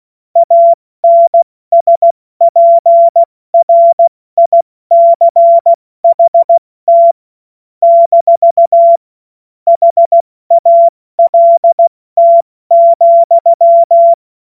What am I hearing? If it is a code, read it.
Morse code, heard as a keyed tone, about 16 wpm, spaced as for 14 wpm, 680 Hz. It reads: ANSPRICHT - HALT,
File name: Morsekurs / Witze / Witz 15. November 2022 70 F80